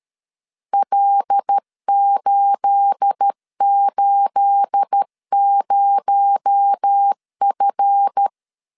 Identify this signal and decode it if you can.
Morse code, heard as a keyed tone, about 13 wpm, 800 Hz. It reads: L880F
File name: CW / DET / 1 9 1500C S C 12wpm 3811